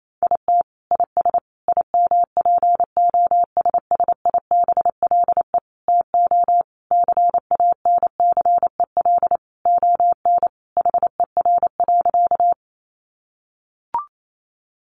Morse code, heard as a keyed tone, about 28 wpm, 700 Hz. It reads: IT IS IMPOSSIBLE TO CANCEL ON HER.
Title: Sentences from Top 1200 Words 28wpm